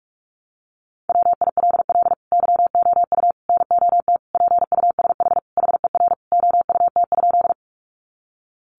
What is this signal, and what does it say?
Morse code, heard as a keyed tone, about 44 wpm, 700 Hz. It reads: WILL YOU NOT PUSH HER OUT?
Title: Sentences from Top 1300 Words 45wpm